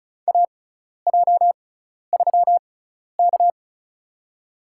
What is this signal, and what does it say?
Morse code, heard as a keyed tone, about 36 wpm, spaced as for 12 wpm, 700 Hz. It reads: AJ3K
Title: USA Call Signs - 4 Characters - 6x Character Spacing - 35wpm